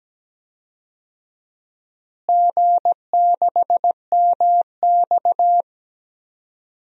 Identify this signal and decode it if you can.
Morse code, heard as a keyed tone, about 17 wpm, 700 Hz. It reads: G6MX
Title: DXCC-100 Call Signs - 4 Characters - Repeated 3x - 17wpm